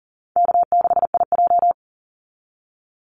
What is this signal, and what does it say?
Morse code, heard as a keyed tone, about 40 wpm, 700 Hz. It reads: K6IJ
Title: Easy Call Signs Repeated 3x - 40wpm